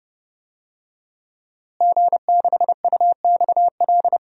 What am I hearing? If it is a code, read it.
Morse code, heard as a keyed tone, about 30 wpm, 700 Hz. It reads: G6UXL
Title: DXCC-100 Call Signs - 5 Characters - Repeated 3x - 30wpm